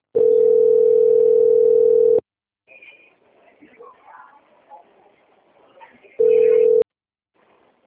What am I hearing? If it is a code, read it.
Ringback tone (North America), 440 + 480 Hz, with speech.
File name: out- - - - - .40014